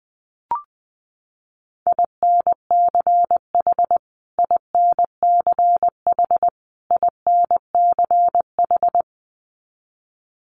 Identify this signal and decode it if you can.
Morse code, heard as a keyed tone, about 20 wpm, 700 Hz. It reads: INCH INCH INCH
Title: Top 500 Words Repeated 20wpm